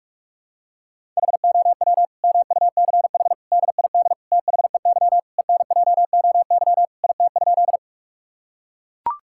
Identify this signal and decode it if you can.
Morse code, heard as a keyed tone, about 41 wpm, 700 Hz. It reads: HOW MUCH DID THEY ENJOY IT?
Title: Sentences from Top 1000 Words - Review Part 5 - 45wpm